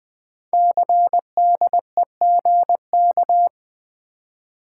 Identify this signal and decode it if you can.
Morse code, heard as a keyed tone, about 21 wpm, 700 Hz. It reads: CDEGK
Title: Code Groups - 5 Letters - 20wpm